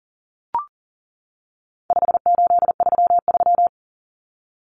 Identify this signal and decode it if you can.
Morse code, heard as a keyed tone, about 40 wpm, 700 Hz. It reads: 5833
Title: Numbers - 4-Digit 40wpm